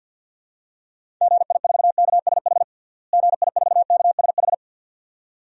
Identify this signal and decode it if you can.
Morse code, heard as a keyed tone, about 50 wpm, 700 Hz. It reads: GI4KSH GI4KSH
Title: DXCC-100 Call Signs - 6 Characters - Repeated 2x - 50wpm